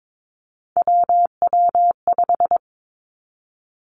Morse code, heard as a keyed tone, about 22 wpm, 700 Hz. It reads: WW5
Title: Three Challenging Letter-Number Combinations 22wpm